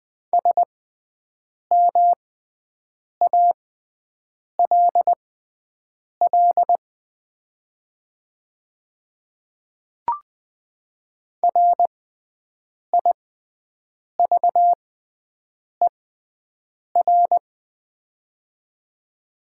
Morse code, heard as a keyed tone, about 20 wpm, spaced as for 7 wpm, 700 Hz. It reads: SMALL RIVER
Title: 6x Character Spacing - Top 300 Words - 20wpm